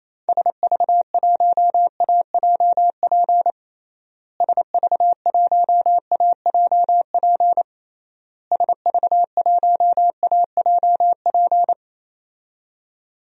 Morse code, heard as a keyed tone, about 28 wpm, 700 Hz. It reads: SV1AJP SV1AJP SV1AJP
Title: DXCC-100 Call Signs - 6 Characters - Repeated 3x - 28wpm